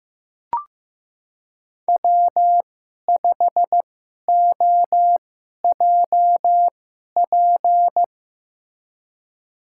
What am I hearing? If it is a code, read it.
Morse code, heard as a keyed tone, about 15 wpm, spaced as for 11 wpm, 700 Hz. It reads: W5OJP
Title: USA Call Signs - 5 Characters - 2x Character Spacing - 15wpm